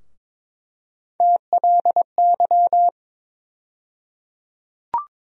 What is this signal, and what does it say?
Morse code, heard as a keyed tone, about 22 wpm, 700 Hz. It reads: TLY
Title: USA Call Sign Postfixes - 3 Letters 22wpm